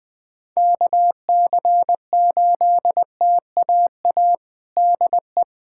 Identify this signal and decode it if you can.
Morse code, heard as a keyed tone, about 21 wpm, 700 Hz. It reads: KC8TAA DE
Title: USA Callsign DE Callsign - 6 Characters - 20wpm